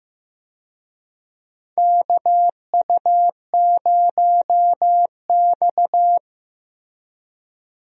Morse code, heard as a keyed tone, about 16 wpm, 700 Hz. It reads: KU0X
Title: USA Call Signs - 4 Characters - Repeated 2x - 15wpm